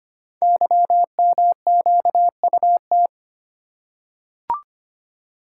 Morse code, heard as a keyed tone, about 25 wpm, 700 Hz. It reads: YMQUT
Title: Code Groups - 5 Letters - 25wpm